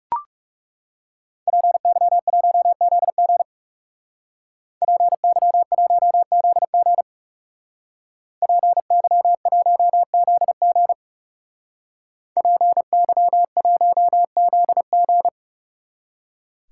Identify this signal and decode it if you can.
Morse code, heard as a keyed tone, about 39 wpm, 700 Hz. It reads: PY1ZG PY1ZG PY1ZG PY1ZG